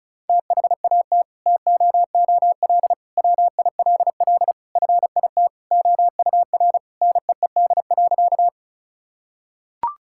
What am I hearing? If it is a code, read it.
Morse code, heard as a keyed tone, about 36 wpm, 700 Hz. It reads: THAT TOOL WILL FIT OUR NEED.
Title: Sentences from Top 1000 Words - Review Part 6 - 35wpm